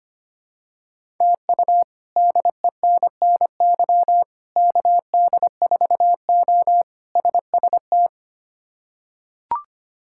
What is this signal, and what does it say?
Morse code, heard as a keyed tone, about 25 wpm, 700 Hz. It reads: TU DENNY KD4O SST
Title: K1USN SST QSO Practice - 25wpm